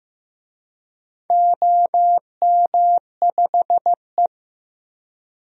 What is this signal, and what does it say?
Morse code, heard as a keyed tone, about 15 wpm, 700 Hz. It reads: OM5E
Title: DXCC-100 Call Signs - 4 Characters - Repeated 3x - 15wpm